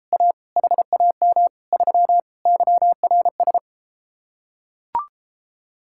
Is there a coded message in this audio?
Yes: Morse code from 0.1 to 3.6 s, heard as a keyed tone, about 33 wpm, 700 Hz. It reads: A HAM 3 YRS